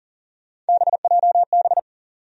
Morse code, heard as a keyed tone, about 40 wpm, 700 Hz. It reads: BJB
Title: USA Call Sign Postfixes - 3 Letters 40wpm